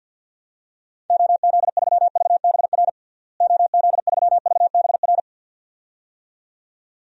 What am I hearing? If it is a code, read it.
Morse code, heard as a keyed tone, about 52 wpm, 700 Hz. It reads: OZ3VBR OZ3VBR
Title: DXCC-100 Call Signs - 6 Characters - Repeated 2x - 50wpm